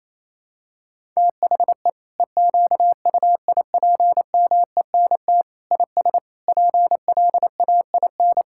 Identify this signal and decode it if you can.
Morse code, heard as a keyed tone, about 28 wpm, 700 Hz. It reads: THE EQUIPMENT IS PLAIN
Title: Sentences from Top 1000 Words - Review Part 5 - 28wpm